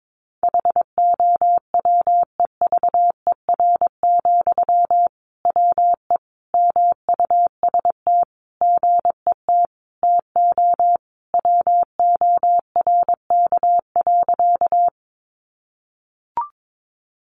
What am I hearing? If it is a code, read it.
Morse code, heard as a keyed tone, about 22 wpm, 700 Hz. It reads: HOWEVER, WE MUST GET TO WORK.
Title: Sentences from Top 600 Words 22wpm